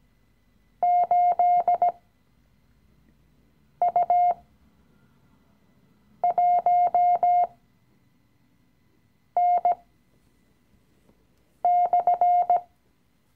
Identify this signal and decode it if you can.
Morse code, heard as a keyed tone, about 18 wpm, spaced as for 4 wpm, 710 Hz. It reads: 8U1N/
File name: CW-Online-Kurs / Anfänger / Lektion 016